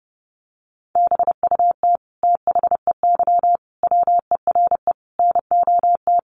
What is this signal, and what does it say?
Morse code, heard as a keyed tone, about 30 wpm, 700 Hz. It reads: BUT THEY WERE NOT
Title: Sentences from Top 1400 Words 30wpm